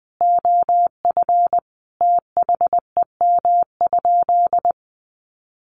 Morse code, heard as a keyed tone, about 19 wpm, 700 Hz. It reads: OF THEM?
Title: Sentences from Top 600 Words 20wpm